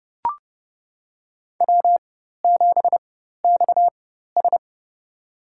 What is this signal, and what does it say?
Morse code, heard as a keyed tone, about 31 wpm, spaced as for 14 wpm, 700 Hz. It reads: W7XS
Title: USA Call Signs - 4x Character Spacing - 30wpm